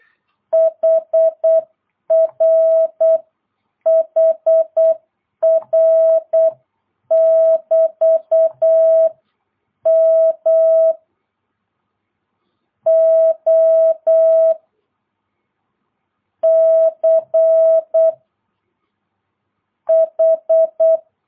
Morse code, heard as a keyed tone, about 8 wpm, 650 Hz. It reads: HR HR <BT> M O C H
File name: jh8cbh / 2022cw